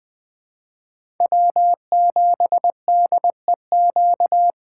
Morse code, heard as a keyed tone, about 20 wpm, 700 Hz. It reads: W7DEQ
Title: USA Call Signs - 3x Character Spacing - 20wpm